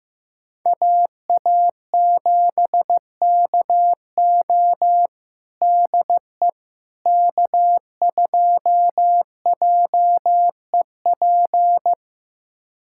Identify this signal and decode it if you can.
Morse code, heard as a keyed tone, about 15 wpm, 700 Hz. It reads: AA7KO DE K2JEP
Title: USA Callsign DE Callsign - 5 Characters - 15wpm